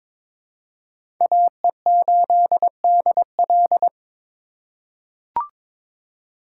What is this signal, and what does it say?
Morse code, heard as a keyed tone, about 22 wpm, 700 Hz. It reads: AE8DL
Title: USA Call Signs - 5 Characters - 4x Character Spacing - 22wpm